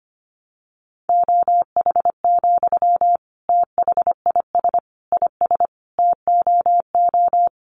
Morse code, heard as a keyed tone, about 25 wpm, 700 Hz. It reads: OH, THIS IS TOO
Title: Sentences from Top 500 Words 25wpm